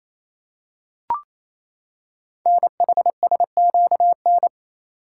Morse code, heard as a keyed tone, about 27 wpm, 700 Hz. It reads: NHSQN
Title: Code Groups - 5 Letter:Numbers - 28wpm